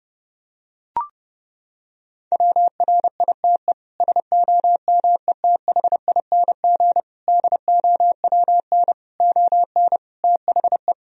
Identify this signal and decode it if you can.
Morse code, heard as a keyed tone, about 31 wpm, 700 Hz. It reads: WRITE SOMETHING DOWN ON THE
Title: Sentences from Top 1000 Words - Review Part 1 - 30wpm